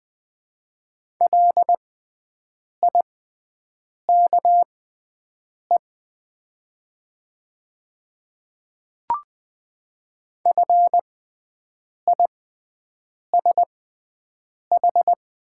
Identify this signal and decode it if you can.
Morse code, heard as a keyed tone, about 20 wpm, spaced as for 7 wpm, 700 Hz. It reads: LIKE FISH
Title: 6x Character Spacing - Top 300 Words - 20wpm